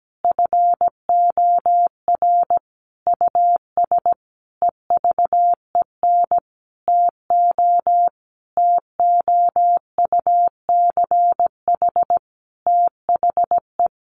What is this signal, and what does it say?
Morse code, heard as a keyed tone, about 17 wpm, 700 Hz. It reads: FOR US EVEN TO TOUCH THE
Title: A Sound Of Thunder 17wpm